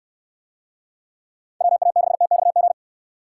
Morse code, heard as a keyed tone, about 68 wpm, 700 Hz. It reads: WA6TXG